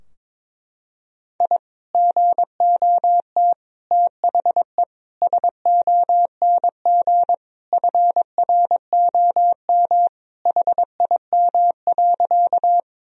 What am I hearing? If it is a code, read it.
Morse code, heard as a keyed tone, about 22 wpm, 700 Hz. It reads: I GOT THE SONG FROM HIM.
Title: Sentences from Top 1000 Words - Review Part 2 - 22wpm